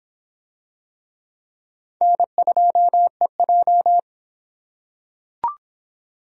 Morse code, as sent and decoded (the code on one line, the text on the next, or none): -. ..--- . .---
N2EJ